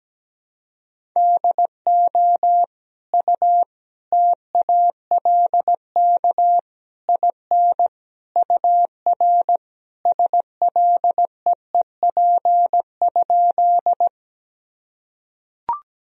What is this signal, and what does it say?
Morse code, heard as a keyed tone, about 17 wpm, 700 Hz. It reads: DO U TALK IN UR SLEEP?
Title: Sentences from Top 500 Words and Abbreviations 17wpm